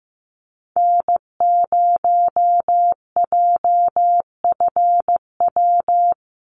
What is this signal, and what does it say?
Morse code, heard as a keyed tone, about 15 wpm, 700 Hz. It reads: N0JFW
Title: USA Call Signs Repeated 2x 15wpm